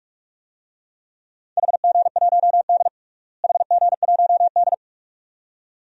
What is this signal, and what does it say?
Morse code, heard as a keyed tone, about 46 wpm, 700 Hz. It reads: HG1D HG1D
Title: DXCC-100 Call Signs - 4 Characters - Repeated 2x - 45wpm